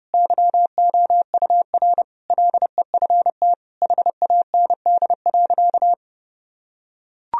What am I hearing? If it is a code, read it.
Morse code, heard as a keyed tone, about 32 wpm, 700 Hz. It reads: YOUR LEFT HAND.
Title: Sentences from Top 1000 Words - Review Part 2 - 30wpm